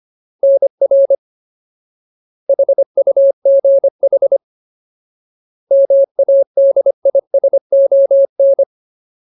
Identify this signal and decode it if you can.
Morse code, heard as a keyed tone, about 25 wpm, 550 Hz. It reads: NR HUGH MADISON
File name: Ultimate Word List / 25WPM Ultimate